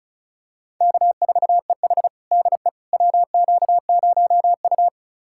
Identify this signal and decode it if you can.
Morse code, heard as a keyed tone, about 36 wpm, 700 Hz. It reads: K4EH DE WQ0U
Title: USA Callsign DE Callsign - 4 Characters - 35wpm